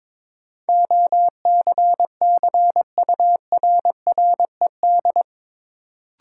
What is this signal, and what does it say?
Morse code, heard as a keyed tone, about 22 wpm, 700 Hz. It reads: OCCURRED
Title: Sets of 5 Words Encore - New Word Drill - 22wpm